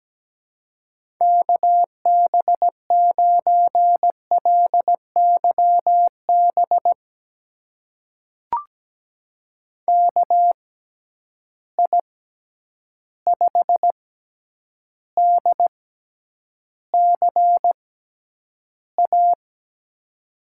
Morse code, heard as a keyed tone, about 17 wpm, spaced as for 9 wpm, 700 Hz. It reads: KB9LYB KI5DCA
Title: USA Call Signs - 6x Character Spacing - 17wpm